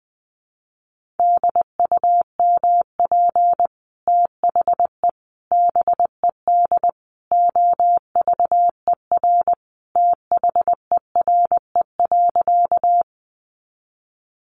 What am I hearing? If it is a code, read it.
Morse code, heard as a keyed tone, about 20 wpm, 700 Hz. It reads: DUMP THE BED OVER THERE.
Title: Sentences from Top 1300 Words 20wpm